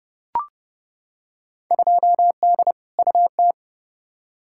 Morse code, heard as a keyed tone, about 30 wpm, 700 Hz. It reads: 2D UT
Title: Field Day Exchanges Encore - 30wpm